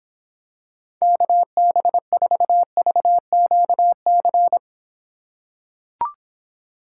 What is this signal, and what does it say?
Morse code, heard as a keyed tone, about 26 wpm, 700 Hz. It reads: KB4VQC